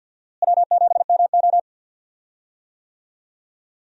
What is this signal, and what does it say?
Morse code, heard as a keyed tone, about 51 wpm, 700 Hz. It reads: W7MO
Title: USA Call Signs - 4 Characters - Repeated 3x - 50wpm